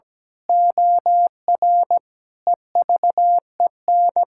Morse code, heard as a keyed tone, about 17 wpm, 700 Hz. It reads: OR EVEN
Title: Pangram Challenge v2 - 17wpm